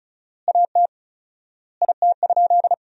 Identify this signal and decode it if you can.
Morse code, heard as a keyed tone, about 36 wpm, 700 Hz. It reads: AT IT?
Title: 4x Word-Spacing - Sentences from Top 100 Words - 35wpm